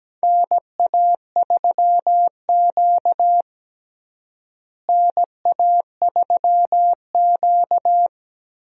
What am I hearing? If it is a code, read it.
Morse code, heard as a keyed tone, about 17 wpm, 700 Hz. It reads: NA3Q NA3Q
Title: USA Call Signs - 4 Characters - Repeated 3x - 17wpm